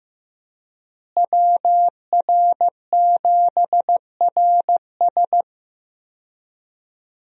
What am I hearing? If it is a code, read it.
Morse code, heard as a keyed tone, about 15 wpm, 700 Hz. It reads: WR7RS
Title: USA Call Signs - 5 Characters - Repeated 3x - 15wpm